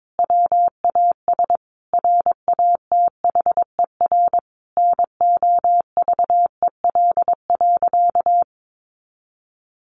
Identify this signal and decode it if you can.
Morse code, heard as a keyed tone, about 22 wpm, 700 Hz. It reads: WAS RATHER NOVEL.
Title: Sentences from Top 1300 Words 22wpm